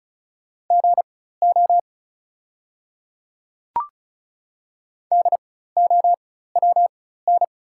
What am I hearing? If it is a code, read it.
Morse code, heard as a keyed tone, about 35 wpm, spaced as for 16 wpm, 690 Hz. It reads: GO DOWN
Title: 4x Character Spacing - Top 100 Words - 35wpm